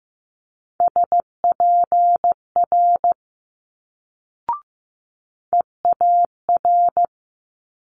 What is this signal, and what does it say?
Morse code, heard as a keyed tone, about 15 wpm, 700 Hz. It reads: SPR EAR
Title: Multi-letter Phonemes Quick Fire 15wpm